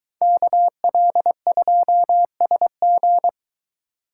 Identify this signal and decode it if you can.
Morse code, heard as a keyed tone, about 22 wpm, 700 Hz. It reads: KL2SG